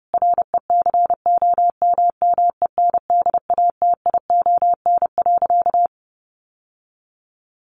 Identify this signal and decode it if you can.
Morse code, heard as a keyed tone, about 28 wpm, 700 Hz. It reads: RECOMMENDATION.
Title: Sentences from Top 1400 Words 30wpm